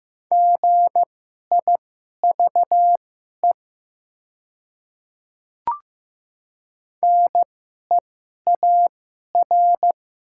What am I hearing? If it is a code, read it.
Morse code, heard as a keyed tone, about 15 wpm, spaced as for 11 wpm, 700 Hz. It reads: GIVE NEAR
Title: 2x Character Spacing - Top 500 Words - 15wpm